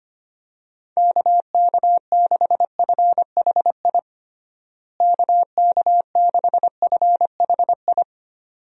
Morse code, heard as a keyed tone, about 25 wpm, 700 Hz. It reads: KK6FHI KK6FHI
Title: USA Call Signs - 6 Characters - Repeated 2x - 25wpm